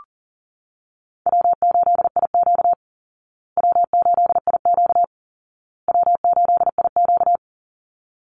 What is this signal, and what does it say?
Morse code, heard as a keyed tone, about 40 wpm, 700 Hz. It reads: W8IQ W8IQ W8IQ
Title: Easy Call Signs Repeated 3x - 40wpm